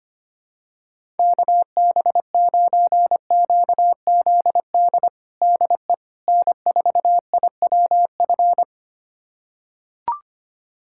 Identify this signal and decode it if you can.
Morse code, heard as a keyed tone, about 25 wpm, 700 Hz. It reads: KB9QZD DE N4IWF
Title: USA Callsign DE Callsign 25wpm